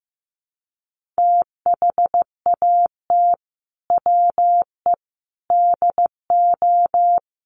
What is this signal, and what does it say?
Morse code, heard as a keyed tone, about 15 wpm, 700 Hz. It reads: THAT WE DO
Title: Sets of 4 Words 15wpm